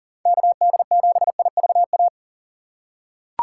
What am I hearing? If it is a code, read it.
Morse code, heard as a keyed tone, about 40 wpm, 700 Hz. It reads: KD7IVA
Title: USA Call Signs - 2x Character Spacing - 40wpm